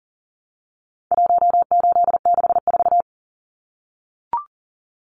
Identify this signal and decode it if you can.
Morse code, heard as a keyed tone, about 40 wpm, 700 Hz. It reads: 1864
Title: Numbers - 4-Digit 40wpm